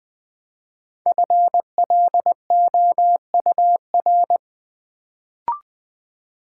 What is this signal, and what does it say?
Morse code, heard as a keyed tone, about 20 wpm, 700 Hz. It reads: FLOUR
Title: Build A Word - 1 Letter At A Time - 5-Letter Words - 20wpm